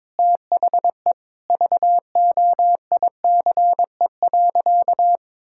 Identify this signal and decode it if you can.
Morse code, heard as a keyed tone, about 22 wpm, 700 Hz. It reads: THE VOICE.
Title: Sentences from Top 1600 Words 22wpm